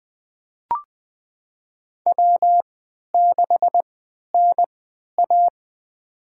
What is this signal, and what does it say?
Morse code, heard as a keyed tone, about 20 wpm, spaced as for 11 wpm, 700 Hz. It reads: W6NA
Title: USA Call Signs - 4 Characters - 3x Character Spacing - 20wpm